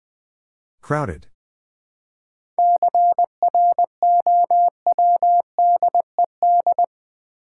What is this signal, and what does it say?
Morse code, heard as a keyed tone, about 20 wpm, 700 Hz. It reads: CROWDED
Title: 7 Letter Words - First 3 Letters Given - 20wpm